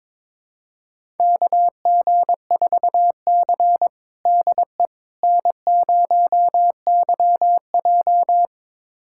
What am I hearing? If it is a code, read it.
Morse code, heard as a keyed tone, about 22 wpm, 700 Hz. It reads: KG4C DE N0YJ
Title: USA Callsign DE Callsign - 4 Characters - 22wpm